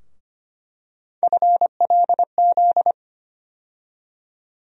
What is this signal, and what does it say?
Morse code, heard as a keyed tone, about 25 wpm, 700 Hz. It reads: FLZ
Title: USA Call Sign Postfixes - 3 Letters 25wpm